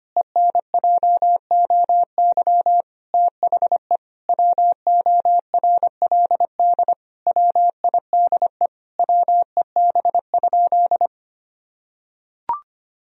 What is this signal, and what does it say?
Morse code, heard as a keyed tone, about 25 wpm, 700 Hz. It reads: ENJOY THE WORLD WIDE WEB?
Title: Sentences from Top 1000 Words - Review Part 4 - 25wpm